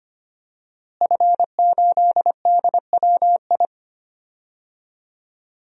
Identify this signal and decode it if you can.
Morse code, heard as a keyed tone, about 25 wpm, 700 Hz. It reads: F8DWI
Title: DXCC-100 Call Signs - 5 Characters - Repeated 3x - 25wpm